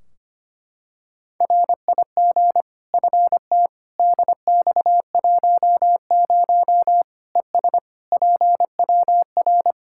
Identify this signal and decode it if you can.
Morse code, heard as a keyed tone, about 25 wpm, 700 Hz. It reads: RIG FT DX10 ES PWR
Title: Practice QSO - Exchange 2 - Info - 25wpm